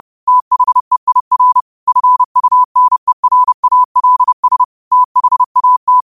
Morse code, heard as a keyed tone, about 30 wpm, 1000 Hz. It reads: THEIR FUNERALS THAT
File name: qotdfiles / 30wpm